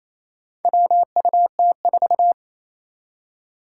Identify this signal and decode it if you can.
Morse code, heard as a keyed tone, about 28 wpm, 700 Hz. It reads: WUT4
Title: Four Challenging Letter-Number Combinations 28wpm